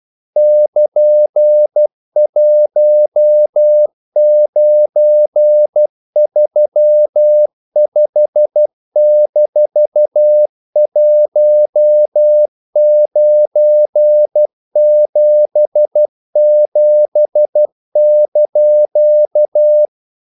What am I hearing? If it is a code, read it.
Morse code, heard as a keyed tone, about 12 wpm, 600 Hz. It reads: <KN>1935-1977)
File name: CW / Practice / EnglishQuotations0312